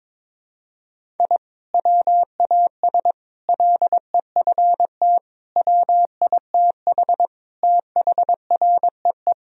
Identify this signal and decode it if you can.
Morse code, heard as a keyed tone, about 21 wpm, 700 Hz. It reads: I WAS LEFT WITH THREE
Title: Sentences from Top 1000 Words - Review Part 1 - 22wpm